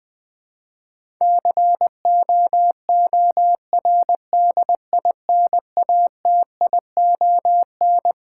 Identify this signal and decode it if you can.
Morse code, heard as a keyed tone, about 20 wpm, 700 Hz. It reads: COORDINATION
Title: Build A Word - 5 Letters At A Time - 12-Letter Words - 20wpm